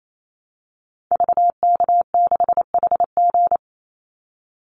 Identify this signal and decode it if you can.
Morse code, heard as a keyed tone, about 28 wpm, 700 Hz. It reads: VK6HG
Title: Call Signs - 28wpm